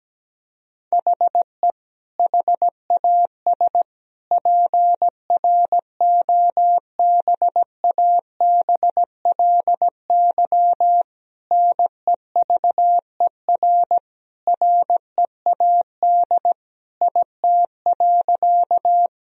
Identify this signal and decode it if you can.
Morse code, heard as a keyed tone, about 17 wpm, 700 Hz. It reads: HE HAS PROBABLY NEVER READ IT.
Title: Sentences from Top 1000 Words - Review Part 4 - 17wpm